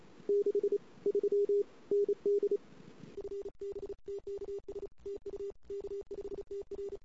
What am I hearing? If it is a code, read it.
Morse code, heard as a keyed tone, about 34 wpm, 410 Hz. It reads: 6 3 ND FBTKS TU K5TR
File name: CW / 2008-ss-cw-k5tr-op-n5rz -22:00